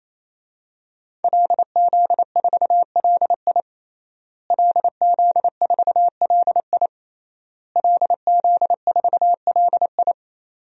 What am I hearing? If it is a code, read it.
Morse code, heard as a keyed tone, about 28 wpm, 700 Hz. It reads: LZ4LI LZ4LI LZ4LI
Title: DXCC-100 Call Signs - 5 Characters - Repeated 3x - 28wpm